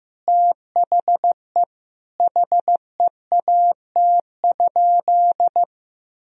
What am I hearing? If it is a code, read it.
Morse code, heard as a keyed tone, about 15 wpm, 700 Hz. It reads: THE HEAT?
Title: Sentences from Top 1000 Words - Review Part 3 - 15wpm